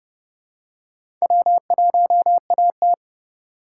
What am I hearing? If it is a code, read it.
Morse code, heard as a keyed tone, about 28 wpm, 700 Hz. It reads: W1AT